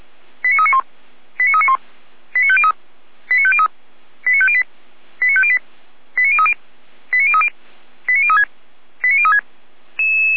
Five-tone selective call (ZVEI), heard as quick runs of five tones, tones of about 70 ms. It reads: 89391 89391 89503 89503 89508 89508 89030 89030 89036 89036